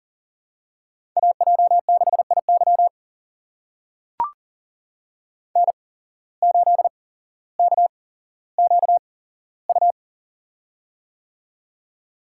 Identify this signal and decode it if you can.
Morse code, heard as a keyed tone, about 40 wpm, spaced as for 17 wpm, 700 Hz. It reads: AJ6IY N8KQU
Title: USA Call Signs - 5 Characters - 8x Character Spacing - 40wpm